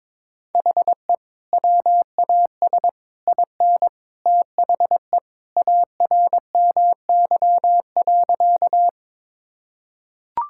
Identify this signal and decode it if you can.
Morse code, heard as a keyed tone, about 22 wpm, 700 Hz. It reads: HE WAS IN THE ARMY.
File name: practice / 22wpm / Sentences with 3-Word Phrases - 22wpm